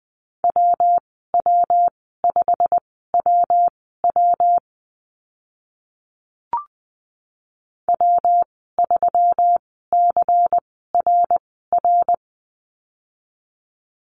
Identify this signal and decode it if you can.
Morse code, heard as a keyed tone, about 20 wpm, spaced as for 15 wpm, 700 Hz. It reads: WW5WW W3CRR
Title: 122 Lesson 20g - Number 3 - Call Signs 20wpm